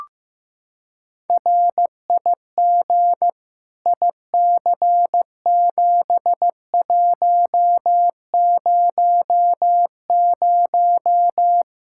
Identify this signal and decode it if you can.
Morse code, heard as a keyed tone, about 15 wpm, 700 Hz. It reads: RIG IC7100
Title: Practice QSO - Exchange 2 - Info - 15wpm